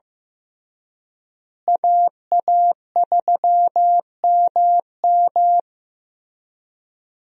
Morse code, heard as a keyed tone, about 15 wpm, 700 Hz. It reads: AA3MM
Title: USA Call Signs - 5 Characters - Repeated 3x - 15wpm